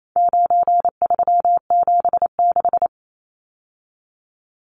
Morse code, heard as a keyed tone, about 28 wpm, 700 Hz. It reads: 9376
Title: Numbers - 4-Digit 28wpm